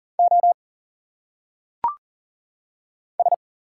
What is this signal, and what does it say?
Morse code, heard as a keyed tone, about 40 wpm, 700 Hz. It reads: O S
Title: Single Letters - Phonetic Pronunciation 40wpm